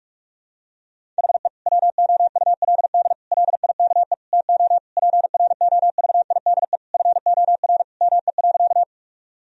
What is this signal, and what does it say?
Morse code, heard as a keyed tone, about 41 wpm, 700 Hz. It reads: HE WOULD LIKE TO PROVIDE FOR ME.
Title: Sentences from Top 1000 Words - Review Part 3 - 45wpm